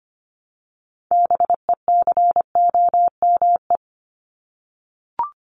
Morse code, heard as a keyed tone, about 25 wpm, 700 Hz. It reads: BECOME
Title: Top 501-750 Words 25wpm